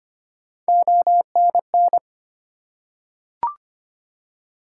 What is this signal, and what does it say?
Morse code, heard as a keyed tone, about 25 wpm, 700 Hz. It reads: ONN
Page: morse-code-ninja-practice-25wpm